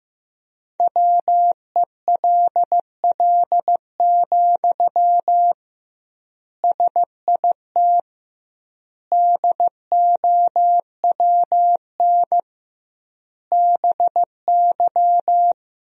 Morse code, heard as a keyed tone, about 15 wpm, 700 Hz. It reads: WELL, SIT DOWN BY
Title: 2x Word-Spacing - Sentences from Top 500 Words - 15wpm